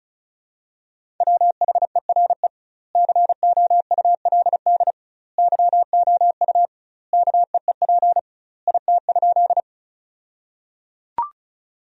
Morse code, heard as a keyed tone, about 36 wpm, 700 Hz. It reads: WHERE COULD YOU KEEP IT?
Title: 2x Word-Spacing - Sentences from Top 300 Words - 35wpm